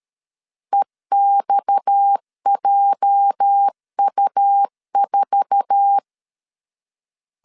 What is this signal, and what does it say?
Morse code, heard as a keyed tone, about 13 wpm, 800 Hz. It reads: EXJU4
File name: LEZIONI / 8 YJ45 / 8 YJ4 12wpm 2151